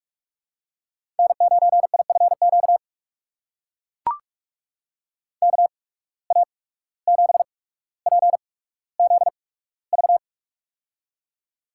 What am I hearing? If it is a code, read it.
Morse code, heard as a keyed tone, about 44 wpm, spaced as for 18 wpm, 700 Hz. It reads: N9IFQ KA7PZV
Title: USA Call Signs - 8x Character Spacing - 45wpm